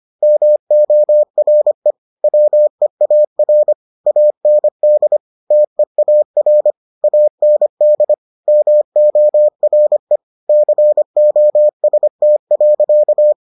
Morse code, heard as a keyed tone, about 25 wpm, 600 Hz. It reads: MORE WEAR AND TEAR AND MORE COST.